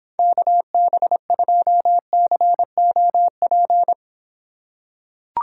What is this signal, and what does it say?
Morse code, heard as a keyed tone, about 26 wpm, 700 Hz. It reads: KB2COP